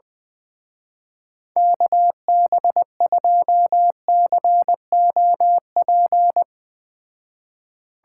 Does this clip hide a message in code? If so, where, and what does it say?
Yes: Morse code from 1.6 to 6.4 s, heard as a keyed tone, about 21 wpm, 700 Hz. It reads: KB2COP